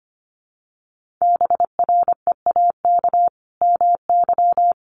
Morse code, heard as a keyed tone, about 25 wpm, 700 Hz. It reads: BREAK MY
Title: Sentences from Top 1400 Words 25wpm